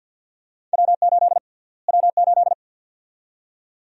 Morse code, heard as a keyed tone, about 51 wpm, 700 Hz. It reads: W8 W8
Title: USA Call Sign Prefixes - Repeated 2x - 50wpm